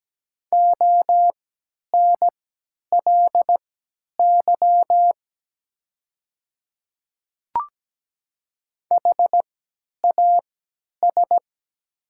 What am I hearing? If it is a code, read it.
Morse code, heard as a keyed tone, about 17 wpm, spaced as for 10 wpm, 700 Hz. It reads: ONLY HAS
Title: 3x Character Spacing - Top 200 Words - 17wpm